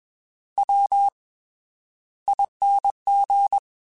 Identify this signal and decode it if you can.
Morse code, heard as a keyed tone, about 21 wpm, 800 Hz. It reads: W ING
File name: TH / TH mp3s / Fast TH @20